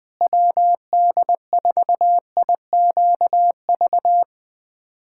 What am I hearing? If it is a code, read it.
Morse code, heard as a keyed tone, about 20 wpm, 700 Hz. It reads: WD4IQV
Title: USA Callsign DE Callsign - 6 Characters - 20wpm